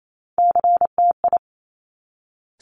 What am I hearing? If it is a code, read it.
Morse code, heard as a keyed tone, about 28 wpm, 700 Hz. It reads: CTI
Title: Top Three Letter Combinations in Words 28wpm